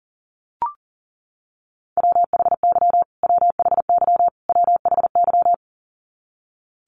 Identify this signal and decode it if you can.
Morse code, heard as a keyed tone, about 40 wpm, 700 Hz. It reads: WHY WHY WHY
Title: Top 500 Words Repeated 40wpm